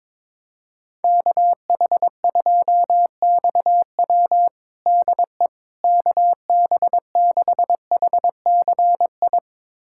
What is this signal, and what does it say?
Morse code, heard as a keyed tone, about 22 wpm, 700 Hz. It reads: KH2XW DE KB6HCI
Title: USA Callsign DE Callsign 22wpm